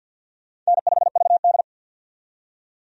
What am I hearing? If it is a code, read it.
Morse code, heard as a keyed tone, about 45 wpm, 700 Hz. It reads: N5VD
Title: USA Call Signs - 4 Characters - 50wpm